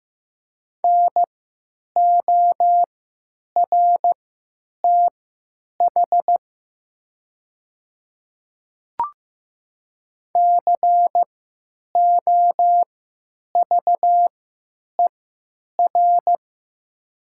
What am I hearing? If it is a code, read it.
Morse code, heard as a keyed tone, about 15 wpm, spaced as for 9 wpm, 700 Hz. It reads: NORTH COVER
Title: 3x Character Spacing - Top 500 Words - 15wpm